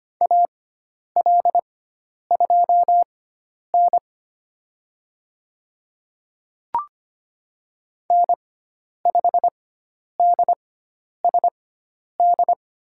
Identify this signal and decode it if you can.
Morse code, heard as a keyed tone, about 25 wpm, spaced as for 10 wpm, 700 Hz. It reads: AL2N N5DSD
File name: Lesson 16g - Letter c - Call Signs 25wpm